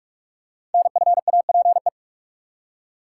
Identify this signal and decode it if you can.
Morse code, heard as a keyed tone, about 44 wpm, 690 Hz. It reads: NFAPE
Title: Code Groups - 5 Letters - 45wpm